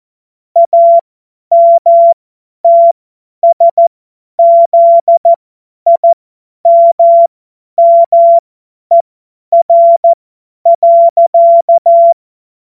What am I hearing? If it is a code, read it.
Morse code, heard as a keyed tone, about 14 wpm, spaced as for 10 wpm, 680 Hz. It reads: AMTSZIMMER.
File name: Witz-07-Oktober-2025-50-F70